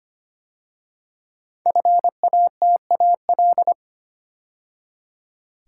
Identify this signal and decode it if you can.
Morse code, heard as a keyed tone, about 25 wpm, 700 Hz. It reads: FATAL
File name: 5 Letter Words - First 3 Letters Given - 25wpm